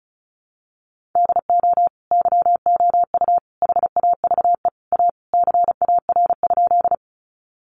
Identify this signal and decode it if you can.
Morse code, heard as a keyed tone, about 34 wpm, 700 Hz. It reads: DO YOU HAVE A CAR?
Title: Sentences from Top 300 Words 35wpm